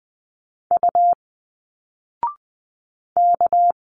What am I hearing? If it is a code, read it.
Morse code, heard as a keyed tone, about 20 wpm, 700 Hz. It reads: U K
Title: Single Letters 20wpm